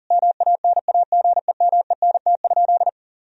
Morse code, heard as a keyed tone, about 40 wpm, 700 Hz. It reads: MANAGEMENT?
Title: Sentences from Top 1000 Words - Review Part 4 - 40wpm